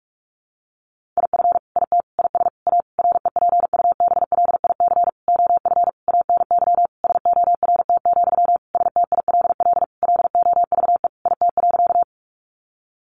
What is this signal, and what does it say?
Morse code, heard as a keyed tone, about 44 wpm, 700 Hz. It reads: IF IT IS A REPUBLIC OF ANY SORT, STILL LOVE IT.
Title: Sentences from Top 1500 Words 45wpm